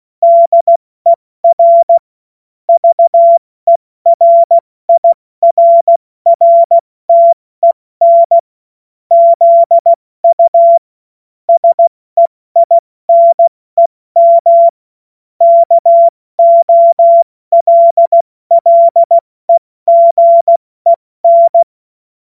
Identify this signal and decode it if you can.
Morse code, heard as a keyed tone, about 16 wpm, 680 Hz. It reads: DER VERIRRTEN ZU SEINEM KOLLEGEN
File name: Morsekurs / Witze / Witz 03. Mai 2022 70 F80